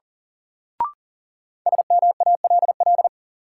Morse code, heard as a keyed tone, about 37 wpm, 700 Hz. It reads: SMALL
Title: Sentences from Top 200 Words - New Word Drill - 40wpm